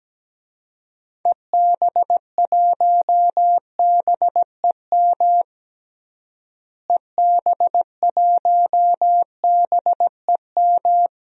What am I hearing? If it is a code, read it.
Morse code, heard as a keyed tone, about 17 wpm, 700 Hz. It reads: EB1BEM EB1BEM
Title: DXCC-100 Call Signs - 6 Characters - Repeated 3x - 17wpm